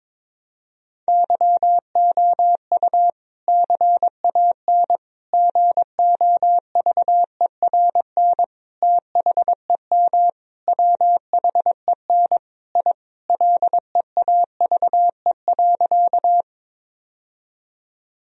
Morse code, heard as a keyed tone, about 22 wpm, 700 Hz. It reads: YOU CAN GOVERN THEM WHEN I LEAVE.
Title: Sentences from Top 1000 Words - Review Part 2 - 22wpm